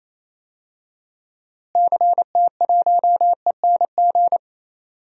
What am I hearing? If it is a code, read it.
Morse code, heard as a keyed tone, about 28 wpm, 700 Hz. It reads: CT1ENG
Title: DXCC-100 Call Signs - 6 Characters - Repeated 2x - 28wpm